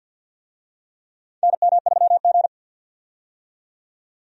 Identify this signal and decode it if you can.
Morse code, heard as a keyed tone, about 52 wpm, 700 Hz. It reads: NM3G
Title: USA Call Signs - 4 Characters - 50wpm